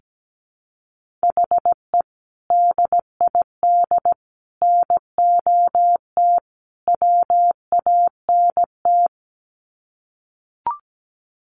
Morse code, heard as a keyed tone, about 17 wpm, 700 Hz. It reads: HE DID NOT WANT
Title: Sets of 4 Words 17wpm